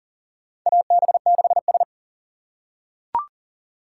Morse code, heard as a keyed tone, about 40 wpm, 700 Hz. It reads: AB6S
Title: USA Call Signs - 4 Characters - 4x Character Spacing - 40wpm